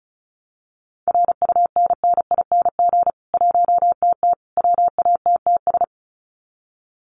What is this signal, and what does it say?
Morse code, heard as a keyed tone, about 34 wpm, 700 Hz. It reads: RUNNING 1TT WATTS
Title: Common QSO Phrases 35wpm